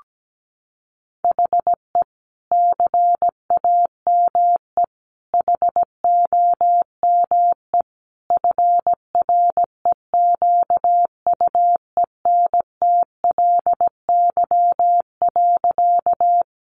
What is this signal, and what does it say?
Morse code, heard as a keyed tone, about 17 wpm, 700 Hz. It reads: HE CAME HOME FREQUENTLY.
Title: Sentences from Top 1300 Words 17wpm